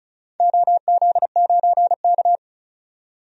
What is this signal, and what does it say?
Morse code, heard as a keyed tone, about 36 wpm, 690 Hz. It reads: OZ9K
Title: DXCC-100 Call Signs - 4 Characters - 35wpm